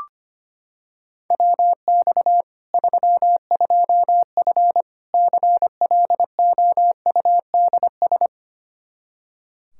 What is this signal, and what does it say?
Morse code, heard as a keyed tone, about 25 wpm, 700 Hz. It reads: WX 32F CLOUDS
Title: Practice QSO - Exchange 2 - Info - 25wpm